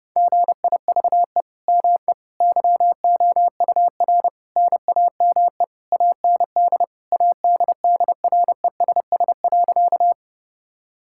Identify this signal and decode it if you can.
Morse code, heard as a keyed tone, about 30 wpm, 700 Hz. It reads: GIVE ME YOUR NAME AND ADDRESS.
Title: Sentences from Top 1000 Words - Review Part 6 - 30wpm